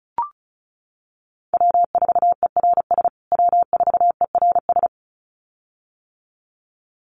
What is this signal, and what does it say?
Morse code, heard as a keyed tone, about 34 wpm, 700 Hz. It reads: W4ERS W4ERS
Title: USA Call Signs Repeated 2x 35wpm